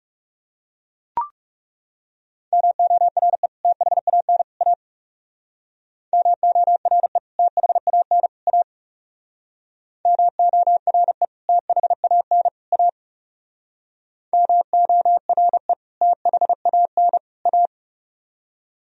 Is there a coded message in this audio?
Yes: Morse code from 2.5 to 17.7 s, heard as a keyed tone, about 38 wpm, 700 Hz. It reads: MORE THAN A MORE THAN A MORE THAN A MORE THAN A